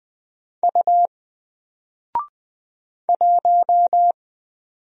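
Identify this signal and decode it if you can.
Morse code, heard as a keyed tone, about 20 wpm, 700 Hz. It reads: U 1
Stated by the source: Single letter and number Morse drill at 20 wpm, from N4LCD.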